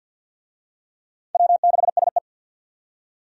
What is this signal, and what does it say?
Morse code, heard as a keyed tone, about 53 wpm, 700 Hz. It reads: W6SE
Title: USA Call Signs - 4 Characters - 50wpm